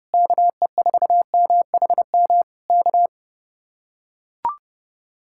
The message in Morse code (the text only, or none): KE4MHM K